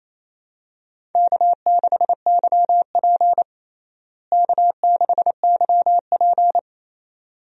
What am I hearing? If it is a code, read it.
Morse code, heard as a keyed tone, about 28 wpm, 700 Hz. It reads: K6YP K6YP
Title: USA Call Signs - 4 Characters - Repeated 2x - 28wpm